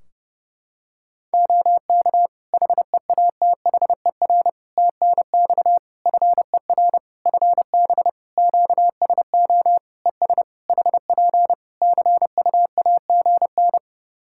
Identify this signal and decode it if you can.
Morse code, heard as a keyed tone, about 30 wpm, 700 Hz. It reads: OK HEATHER TNX FER FB QSO ES HP CUAGN
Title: K1USN SST QSO Practice 30wpm